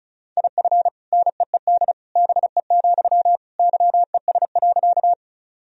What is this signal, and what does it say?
Morse code, heard as a keyed tone, about 36 wpm, 700 Hz. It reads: IF NEED BE, YES.
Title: Sentences from Top 1000 Words - Review Part 1 - 35wpm